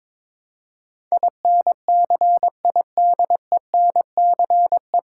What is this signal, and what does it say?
Morse code, heard as a keyed tone, about 22 wpm, 700 Hz. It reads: INCIDENCE
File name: 9 Letter Words - First 3 Letters Given - 22wpm